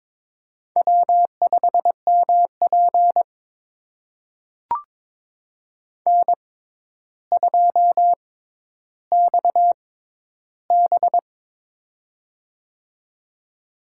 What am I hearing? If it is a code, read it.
Morse code, heard as a keyed tone, about 23 wpm, spaced as for 11 wpm, 700 Hz. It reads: W5MP N2XB
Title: USA Call Signs - 4 Characters - 6x Character Spacing - 22wpm